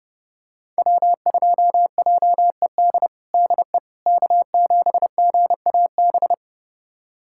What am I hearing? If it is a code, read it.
Morse code, heard as a keyed tone, about 32 wpm, 700 Hz. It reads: W2JED DE K7GAB
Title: USA Callsign DE Callsign - 5 Characters - 30wpm